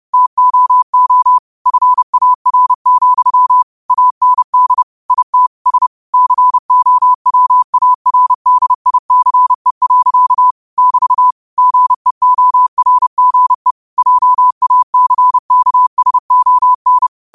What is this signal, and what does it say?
Morse code, heard as a keyed tone, about 30 wpm, 1000 Hz. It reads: TOO FAR, AND IT S COWARDICE. <BT> GEORGE JACKSON
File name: qotdfiles / 30wpm